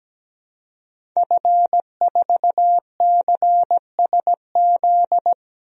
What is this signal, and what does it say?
Morse code, heard as a keyed tone, about 17 wpm, 700 Hz. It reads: F4CSZ
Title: DXCC-100 Call Signs - 5 Characters - 17wpm